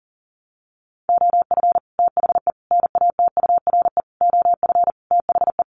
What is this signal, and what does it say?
Morse code, heard as a keyed tone, about 40 wpm, 700 Hz. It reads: OF THE NATURE OF THE
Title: Sets of 5 Words Encore2 40wpm